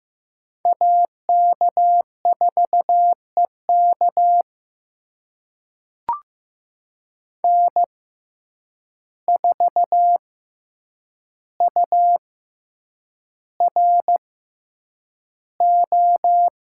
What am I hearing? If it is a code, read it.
Morse code, heard as a keyed tone, about 15 wpm, spaced as for 8 wpm, 700 Hz. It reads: AK4EK N4URO
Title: USA Call Signs - 5 Characters - 6x Character Spacing - 15wpm